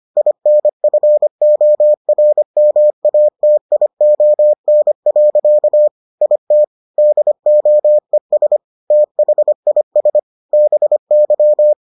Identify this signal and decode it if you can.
Morse code, heard as a keyed tone, about 25 wpm, 600 Hz. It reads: INFORMATION. IT DOES THIS BY